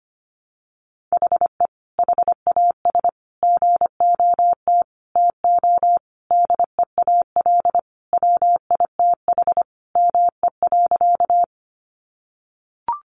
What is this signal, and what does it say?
Morse code, heard as a keyed tone, about 25 wpm, 700 Hz. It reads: HE HAS GOT TO DEAL WITH ME.
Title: Sentences from Top 800 Words 25wpm